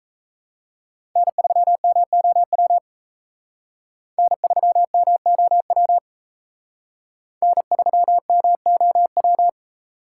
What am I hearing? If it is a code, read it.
Morse code, heard as a keyed tone, about 35 wpm, 700 Hz. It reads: N3MOW N3MOW N3MOW
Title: Speed Racing - USA 1x3 Call Signs 28wpm